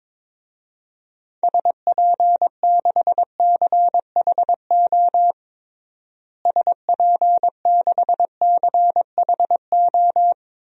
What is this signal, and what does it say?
Morse code, heard as a keyed tone, about 23 wpm, 700 Hz. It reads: SP6CHO SP6CHO
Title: DXCC-100 Call Signs - 6 Characters - Repeated 3x - 22wpm